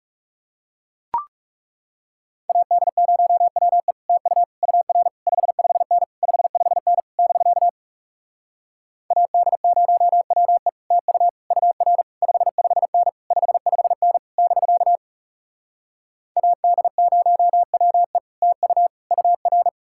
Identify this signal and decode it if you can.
Morse code, heard as a keyed tone, about 40 wpm, 700 Hz. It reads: AD0WE TU UR 55N 55N <BK> AD0WE TU UR 55N 55N <BK> AD0WE TU UR